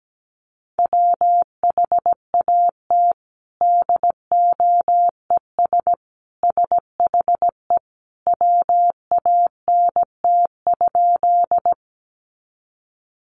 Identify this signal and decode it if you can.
Morse code, heard as a keyed tone, about 17 wpm, 700 Hz. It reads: WHAT DOES SHE WANT?
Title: Sentences from Top 200 Words 17wpm